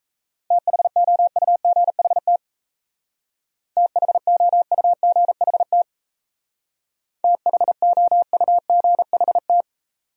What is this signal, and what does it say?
Morse code, heard as a keyed tone, about 34 wpm, 700 Hz. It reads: THOUGHT THOUGHT THOUGHT
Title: Speed-Racing Top 500 Words 28wpm